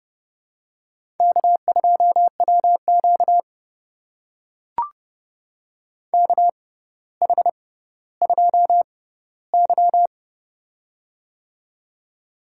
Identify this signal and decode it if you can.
Morse code, heard as a keyed tone, about 30 wpm, spaced as for 15 wpm, 700 Hz. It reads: K2WQ KH2Y
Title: USA Call Signs - 4 Characters - 6x Character Spacing - 30wpm